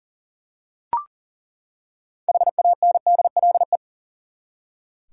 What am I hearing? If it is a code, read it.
Morse code, heard as a keyed tone, about 37 wpm, 700 Hz. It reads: HANDLE
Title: Six Letter Words - 40wpm